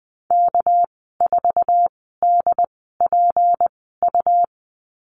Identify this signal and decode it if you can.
Morse code, heard as a keyed tone, about 20 wpm, spaced as for 14 wpm, 700 Hz. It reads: K4DPU
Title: 200 Lesson 32g - Letter k - Call Signs 20wpm